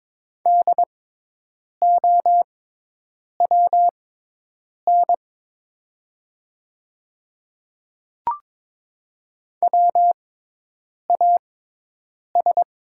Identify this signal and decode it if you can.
Morse code, heard as a keyed tone, about 22 wpm, spaced as for 8 wpm, 700 Hz. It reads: DOWN WAS
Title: 6x Character Spacing - Top 200 Words - 22wpm